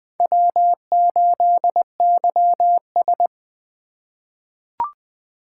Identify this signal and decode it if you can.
Morse code, heard as a keyed tone, about 20 wpm, 700 Hz. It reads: W8YS
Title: USA Call Signs - 4 Characters - 8x Character Spacing - 20wpm